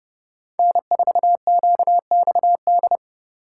Morse code, heard as a keyed tone, about 30 wpm, 700 Hz. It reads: N4QXD
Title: USA Call Signs - 5 Characters - 3x Character Spacing - 30wpm